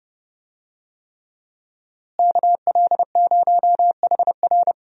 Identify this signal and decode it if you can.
Morse code, heard as a keyed tone, about 30 wpm, 700 Hz. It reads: KL0HR
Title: USA Call Signs - 5 Characters - Repeated 3x - 30wpm